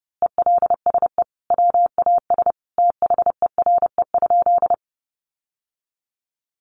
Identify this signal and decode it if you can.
Morse code, heard as a keyed tone, about 30 wpm, 700 Hz. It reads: ELSE WAS THERE?
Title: Sentences from Top 800 Words 30wpm